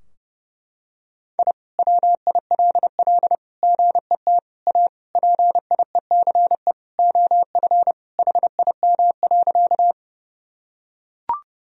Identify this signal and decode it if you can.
Morse code, heard as a keyed tone, about 32 wpm, 700 Hz. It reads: I WILL GET A PIECE OF HIM.
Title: Sentences from Top 1000 Words - Review Part 2 - 30wpm